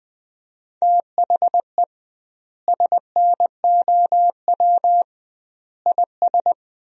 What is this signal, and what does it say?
Morse code, heard as a keyed tone, about 20 wpm, 700 Hz. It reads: THE SNOW IS
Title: 2x Word-Spacing - Sentences from Top 500 Words - 20wpm